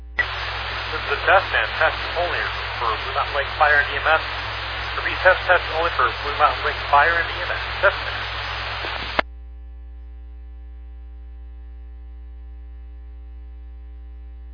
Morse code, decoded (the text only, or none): none